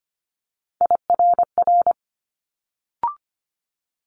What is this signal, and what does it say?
Morse code, heard as a keyed tone, about 25 wpm, 700 Hz. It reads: IRR
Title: Multi-letter Phonemes Quick Fire 25wpm